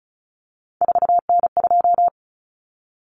Morse code, heard as a keyed tone, about 34 wpm, 700 Hz. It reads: 4N2
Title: Common Cut Numbers - 3-Digit 35wpm